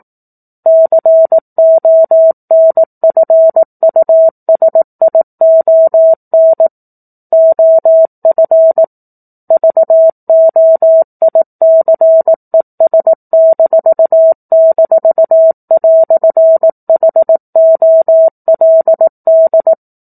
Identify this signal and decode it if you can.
Morse code, heard as a keyed tone, about 18 wpm, 650 Hz. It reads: CONFUSION OF VOICES--"HOLD
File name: 04-Chapter-04 18 WPM